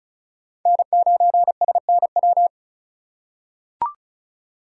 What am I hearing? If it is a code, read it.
Morse code, heard as a keyed tone, about 36 wpm, 700 Hz. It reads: N9SNW
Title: USA Call Signs - 5 Characters - 3x Character Spacing - 35wpm